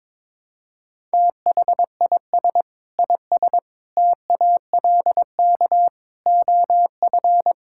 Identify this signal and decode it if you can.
Morse code, heard as a keyed tone, about 21 wpm, 700 Hz. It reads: THIS IS TALK OF
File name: practice / 22wpm / Build A Phrase - 5-Word Phrases - 22wpm